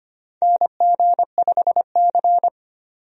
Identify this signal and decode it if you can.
Morse code, heard as a keyed tone, about 25 wpm, 700 Hz. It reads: NG5C
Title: USA Call Signs - 4 Characters - 25wpm